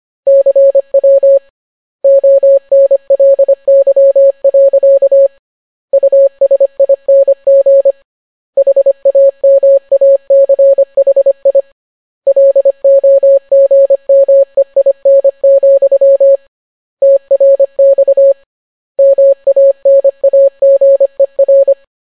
Morse code, heard as a keyed tone, about 25 wpm, 550 Hz. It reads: CW ONLY. USING HAMACHI LOGMEIN, TRX MANAGER